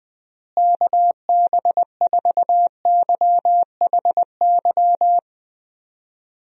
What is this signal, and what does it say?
Morse code, heard as a keyed tone, about 21 wpm, 700 Hz. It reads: KB4YHY
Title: USA Call Signs - 6 Characters - Repeated 3x - 20wpm